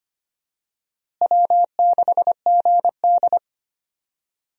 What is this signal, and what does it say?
Morse code, heard as a keyed tone, about 25 wpm, 700 Hz. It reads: W6GD
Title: USA Call Signs - 4 Characters - Repeated 2x - 25wpm